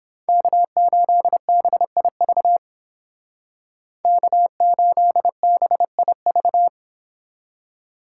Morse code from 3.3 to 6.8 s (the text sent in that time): K8BIV